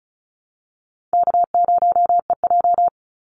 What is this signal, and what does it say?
Morse code, heard as a keyed tone, about 34 wpm, 700 Hz. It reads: K0EJ
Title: Easy Call Signs Repeated 3x - 35wpm